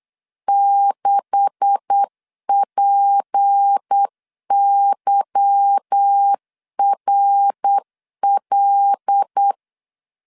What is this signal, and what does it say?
Morse code, heard as a keyed tone, about 8 wpm, 800 Hz. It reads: 6PYRL
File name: LEZIONI / 9 286 8wpm 3425